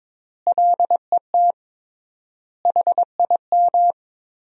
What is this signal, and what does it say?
Morse code, heard as a keyed tone, about 22 wpm, 700 Hz. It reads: LET HIM
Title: 3x Word-Spacing - Sentences from Top 300 Words - 22wpm